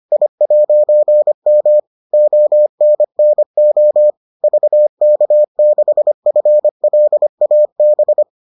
Morse code, heard as a keyed tone, about 25 wpm, 600 Hz. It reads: I'M ONNO VK6FLAB